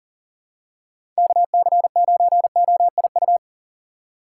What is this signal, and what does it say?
Morse code, heard as a keyed tone, about 39 wpm, 700 Hz. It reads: KC9OIU
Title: USA Call Signs - 6 Characters - 3x Character Spacing - 40wpm